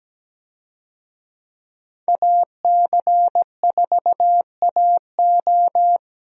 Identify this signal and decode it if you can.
Morse code, heard as a keyed tone, about 17 wpm, 700 Hz. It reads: AC4AO
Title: USA Call Signs - 5 Characters - 17wpm